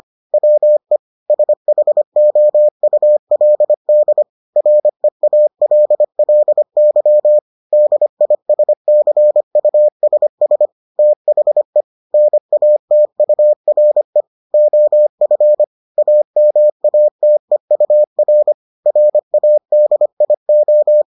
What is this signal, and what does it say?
Morse code, heard as a keyed tone, about 25 wpm, 600 Hz. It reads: WE SHOULD REALLY DISCUSS THE NATURE OF AMATEUR RADIO